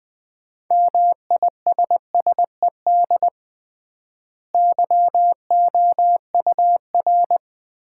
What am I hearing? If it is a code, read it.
Morse code, heard as a keyed tone, about 20 wpm, 700 Hz. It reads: MISSED YOUR
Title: Common QSO Phrases - 3x Word-Spacing - 20wpm